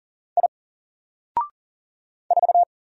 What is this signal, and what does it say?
Morse code, heard as a keyed tone, about 40 wpm, 700 Hz. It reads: I 4
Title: Single Letter-Number 40wpm